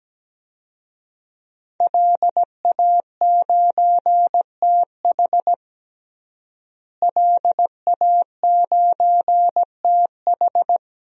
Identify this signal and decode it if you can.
Morse code, heard as a keyed tone, about 17 wpm, 700 Hz. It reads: LA9TH LA9TH
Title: DXCC-100 Call Signs - 5 Characters - Repeated 3x - 17wpm